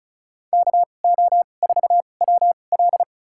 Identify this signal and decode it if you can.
Morse code, heard as a keyed tone, about 36 wpm, spaced as for 25 wpm, 700 Hz. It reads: KO4WL
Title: USA Call Signs - 5 Characters - 2x Character Spacing - 35wpm